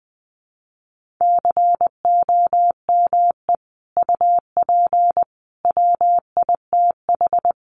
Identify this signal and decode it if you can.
Morse code, heard as a keyed tone, about 20 wpm, 700 Hz. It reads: COME UP WITH
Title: Sets of 4 Words 20wpm